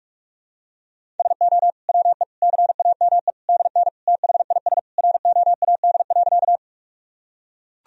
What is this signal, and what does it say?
Morse code, heard as a keyed tone, about 42 wpm, 700 Hz. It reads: SO WE CAME DN THIS ROAD.
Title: Sentences from Top 400 Words and Abbreviations 45wpm